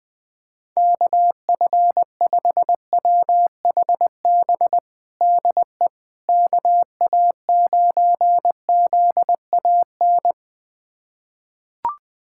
Morse code, heard as a keyed tone, about 21 wpm, 700 Hz. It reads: KF5WHB DE KA9ZAN
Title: USA Callsign DE Callsign - 6 Characters - 20wpm